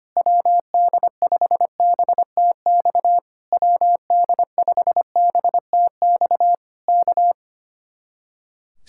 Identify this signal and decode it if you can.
Morse code, heard as a keyed tone, about 24 wpm, 700 Hz. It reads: WD5BTX WD5BTX K
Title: Speed-Racing Common QSO Phrases - 25wpm